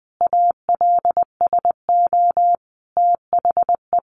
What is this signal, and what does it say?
Morse code, heard as a keyed tone, about 20 wpm, 700 Hz. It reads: ALSO THE
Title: Sentences from Top 1400 Words 20wpm